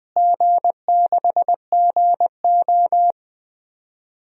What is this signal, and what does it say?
Morse code, heard as a keyed tone, about 21 wpm, 700 Hz. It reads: G6GO
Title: DXCC-100 Call Signs - 4 Characters - Repeated 3x - 20wpm